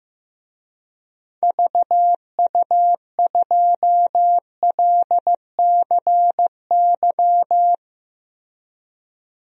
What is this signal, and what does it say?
Morse code, heard as a keyed tone, about 15 wpm, 700 Hz. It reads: VU2LCY
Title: DXCC-100 Call Signs - 6 Characters - Repeated 2x - 15wpm